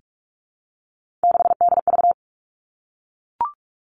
Morse code, heard as a keyed tone, about 44 wpm, 700 Hz. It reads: 6DV
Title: All Cut Numbers - 3-Digit 45wpm